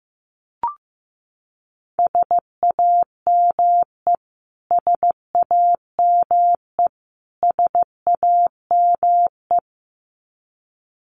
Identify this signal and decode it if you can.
Morse code, heard as a keyed tone, about 15 wpm, 700 Hz. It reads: SAME SAME SAME
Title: Top 500 Words Repeated 15wpm